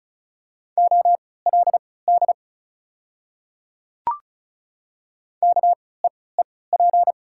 Morse code, heard as a keyed tone, about 36 wpm, spaced as for 20 wpm, 700 Hz. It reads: OLD KEEP
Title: 3x Character Spacing - Top 500 Words - 35wpm